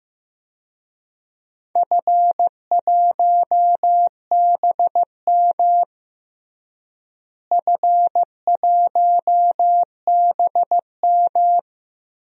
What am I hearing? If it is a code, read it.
Morse code, heard as a keyed tone, about 15 wpm, 700 Hz. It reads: F1BM F1BM
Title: DXCC-100 Call Signs - 4 Characters - Repeated 2x - 15wpm